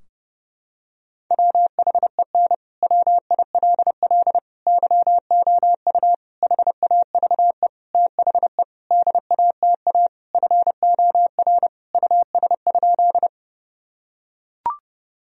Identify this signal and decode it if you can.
Morse code, heard as a keyed tone, about 30 wpm, 700 Hz. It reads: WHEN WILL YOU HAVE THE DATA FOR US?
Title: Sentences from Top 1000 Words - Review Part 3 - 30wpm